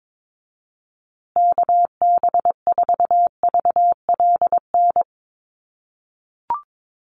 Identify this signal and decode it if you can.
Morse code, heard as a keyed tone, about 22 wpm, 700 Hz. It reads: KB4VLN
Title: USA Call Signs Repeated 2x 22wpm